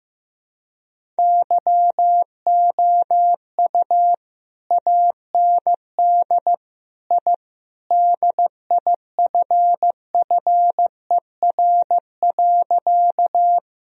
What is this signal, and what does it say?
Morse code, heard as a keyed tone, about 15 wpm, 700 Hz. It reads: YOU AND I DIFFER.
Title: Sentences from Top 1000 Words - Review Part 1 - 15wpm